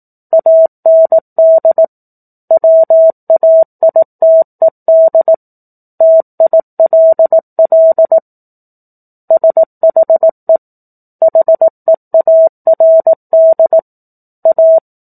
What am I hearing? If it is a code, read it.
Morse code, heard as a keyed tone, about 18 wpm, 650 Hz. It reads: AND WAITED TILL SHE HEARD A
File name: 04-Chapter-04 18 WPM